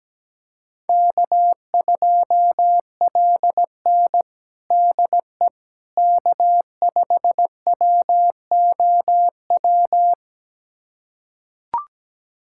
Morse code, heard as a keyed tone, about 17 wpm, 700 Hz. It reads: K2LN DE K5WOW
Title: USA Callsign DE Callsign 17wpm